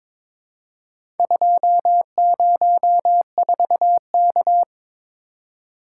Morse code, heard as a keyed tone, about 22 wpm, 700 Hz. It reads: 204K
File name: practice / 22wpm / Four Letter-Number Combinations - 22wpm